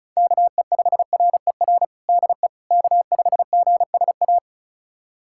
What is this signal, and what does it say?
Morse code, heard as a keyed tone, about 34 wpm, 700 Hz. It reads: KE5RER DE K5GSA
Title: USA Callsign DE Callsign 35wpm